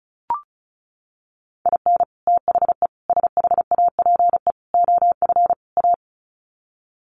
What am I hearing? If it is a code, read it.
Morse code, heard as a keyed tone, about 34 wpm, 700 Hz. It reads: IN THE SHAPE OF A
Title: Sets of 5 Words Encore 35wpm